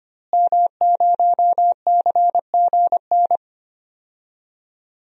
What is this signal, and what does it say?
Morse code, heard as a keyed tone, about 25 wpm, 700 Hz. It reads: M0CGN
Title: DXCC-100 Call Signs - 5 Characters - Repeated 3x - 25wpm